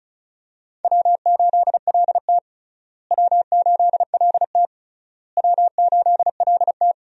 Morse code, heard as a keyed tone, about 36 wpm, 700 Hz. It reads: W8LT W8LT W8LT
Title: USA Call Signs - 4 Characters - Repeated 3x - 35wpm